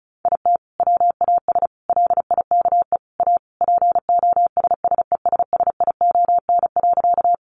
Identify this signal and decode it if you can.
Morse code, heard as a keyed tone, about 34 wpm, 700 Hz. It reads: IT WAS LIKE A POSSESSION.
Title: Sentences from Top 1400 Words 35wpm